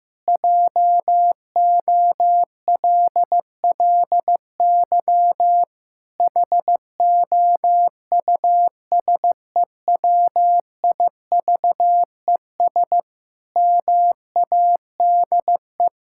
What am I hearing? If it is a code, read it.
Morse code, heard as a keyed tone, about 14 wpm, 700 Hz. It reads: JOLLY HOUSEWIVES MADE
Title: Pangram Challenge v2 - 15wpm